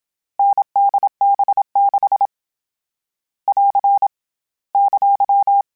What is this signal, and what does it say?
Morse code, heard as a keyed tone, about 27 wpm, 800 Hz. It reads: NDB6 <AR> !
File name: Difficult Combos / Difficult Combos-mp3s / NDB NDB6 @25